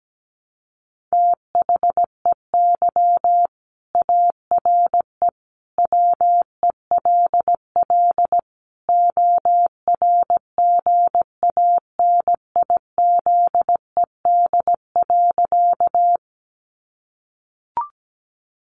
Morse code, heard as a keyed tone, about 17 wpm, 700 Hz. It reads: THEY ARE WELL ORGANIZED.
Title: Sentences from Top 1300 Words 17wpm